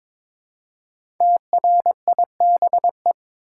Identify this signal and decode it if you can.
Morse code, heard as a keyed tone, about 23 wpm, 700 Hz. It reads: TRIBE
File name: 5 Letter Words - First 3 Letters Given - 22wpm